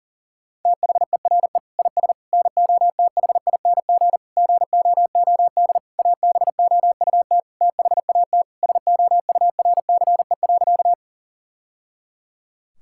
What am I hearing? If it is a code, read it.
Morse code, heard as a keyed tone, about 40 wpm, 700 Hz. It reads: THERE IS NOTHING GOOD ABOUT THAT SOURCE.
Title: Sentences from Top 1000 Words - Review Part 5 - 40wpm